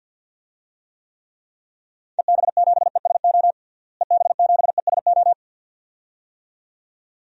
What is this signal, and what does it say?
Morse code, heard as a keyed tone, about 53 wpm, 700 Hz. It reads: EB7ESO EB7ESO
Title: DXCC-100 Call Signs - 6 Characters - Repeated 2x - 50wpm